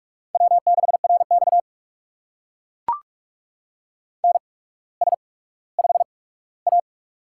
Morse code, heard as a keyed tone, about 44 wpm, spaced as for 19 wpm, 700 Hz. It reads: W6RX NS5A
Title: USA Call Signs - 4 Characters - 8x Character Spacing - 45wpm